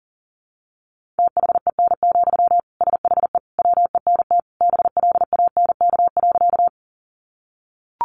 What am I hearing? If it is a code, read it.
Morse code, heard as a keyed tone, about 40 wpm, 700 Hz. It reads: THEN, SHE WENT BLANK.
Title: Sentences from Top 1500 Words 40wpm